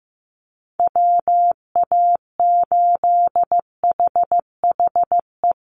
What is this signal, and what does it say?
Morse code, heard as a keyed tone, about 15 wpm, 700 Hz. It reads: WA8HHE
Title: USA Call Signs Repeated 2x 15wpm